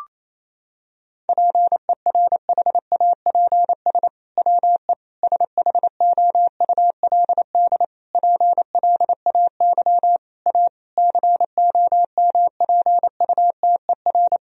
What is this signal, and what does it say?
Morse code, heard as a keyed tone, about 28 wpm, 700 Hz. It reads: PERHAPS WE SHOULD PLAY A COMPUTER
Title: Sentences from Top 1000 Words - Review Part 3 - 28wpm